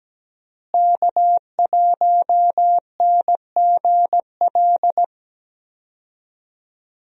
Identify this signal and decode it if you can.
Morse code, heard as a keyed tone, about 17 wpm, 700 Hz. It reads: K1NGL
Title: USA Call Signs - 5 Characters - Repeated 3x - 17wpm